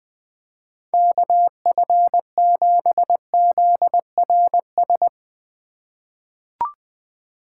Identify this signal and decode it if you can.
Morse code, heard as a keyed tone, about 20 wpm, 700 Hz. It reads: KF7ZRS
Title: USA Call Signs - 6 Characters - 3x Character Spacing - 20wpm